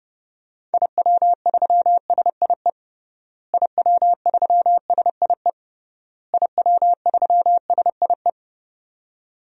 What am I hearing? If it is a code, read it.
Morse code, heard as a keyed tone, about 30 wpm, 700 Hz. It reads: IW3SIE IW3SIE IW3SIE
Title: DXCC-100 Call Signs - 6 Characters - Repeated 3x - 30wpm